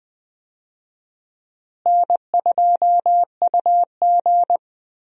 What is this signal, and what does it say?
Morse code, heard as a keyed tone, about 20 wpm, 700 Hz. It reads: N2UG
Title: USA Call Signs - 4 Characters - Repeated 3x - 20wpm